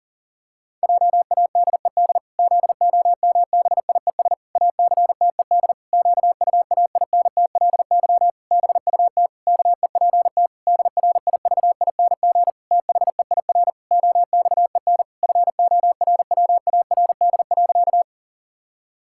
Morse code, heard as a keyed tone, about 37 wpm, 700 Hz. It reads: JADED ZOMBIES ACTED QUAINTLY BUT KEPT DRIVING THEIR OXEN FORWARD.
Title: Pangram Challenge v2 - 40wpm